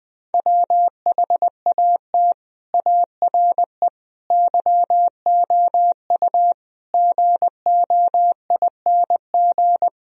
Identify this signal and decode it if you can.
Morse code, heard as a keyed tone, about 20 wpm, 700 Hz. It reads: WHAT ARE YOU GOING
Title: Speed-Racing - Sets of 5 Words 20wpm